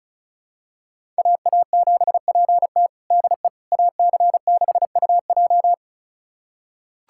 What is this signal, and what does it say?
Morse code, heard as a keyed tone, about 36 wpm, 700 Hz. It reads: AA7PT DE AC6UJ
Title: USA Callsign DE Callsign - 5 Characters - 35wpm